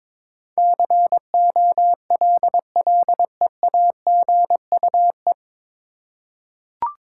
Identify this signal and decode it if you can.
Morse code, heard as a keyed tone, about 21 wpm, 700 Hz. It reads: COLLEAGUE
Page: Morse Code Ninja practice 22wpm